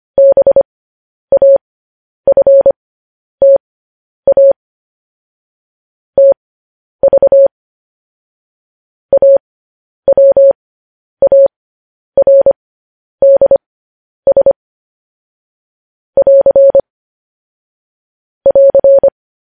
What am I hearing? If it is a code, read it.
Morse code, heard as a keyed tone, about 25 wpm, spaced as for 10 wpm, 550 Hz. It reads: BAFTA TV AWARDS <AR> <AR>
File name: podcast / News headlines-1x-25x10